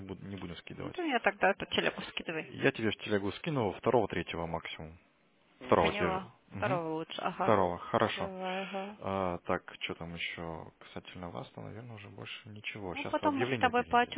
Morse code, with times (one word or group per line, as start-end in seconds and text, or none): none